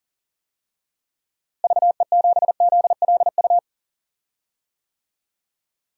Morse code, heard as a keyed tone, about 40 wpm, 700 Hz. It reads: VE7ZLU
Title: DXCC-100 Call Signs - 6 Characters - Repeated 3x - 40wpm